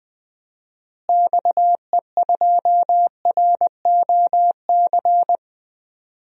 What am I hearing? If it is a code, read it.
Morse code, heard as a keyed tone, about 20 wpm, 700 Hz. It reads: XE2ROC
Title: DXCC-100 Call Signs - 6 Characters - Repeated 2x - 20wpm